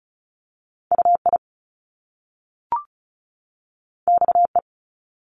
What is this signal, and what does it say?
Morse code, heard as a keyed tone, about 36 wpm, 700 Hz. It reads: UI XE
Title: Top Two Letter Combinations in Words 35wpm